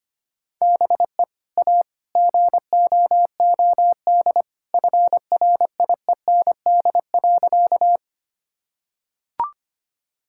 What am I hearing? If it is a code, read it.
Morse code, heard as a keyed tone, about 25 wpm, 700 Hz. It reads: BE A GOOD FRIEND.
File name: practice / 25wpm / Sentences with 2-Word Phrases - 25wpm